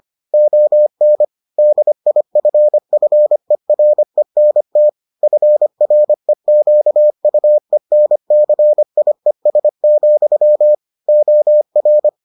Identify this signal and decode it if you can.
Morse code, heard as a keyed tone, about 25 wpm, 600 Hz. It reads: ON DIFFERENT FREQUENCIES, OR